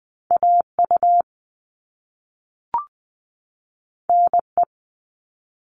Morse code, heard as a keyed tone, about 20 wpm, 700 Hz. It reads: AU NE
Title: All Cut Numbers - 2-Digit 20wpm